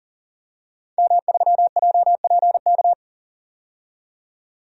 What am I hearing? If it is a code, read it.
Morse code, heard as a keyed tone, about 40 wpm, 700 Hz. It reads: M3JPK
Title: DXCC-100 Call Signs - 5 Characters - 40wpm